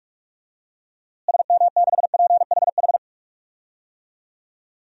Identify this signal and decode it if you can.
Morse code, heard as a keyed tone, about 42 wpm, 700 Hz. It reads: SM6PHH
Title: DXCC-100 Call Signs - 6 Characters - Repeated 2x - 45wpm